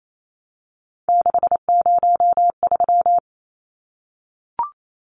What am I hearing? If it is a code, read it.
Morse code, heard as a keyed tone, about 28 wpm, 700 Hz. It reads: 603
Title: Numbers - 3-Digit 28wpm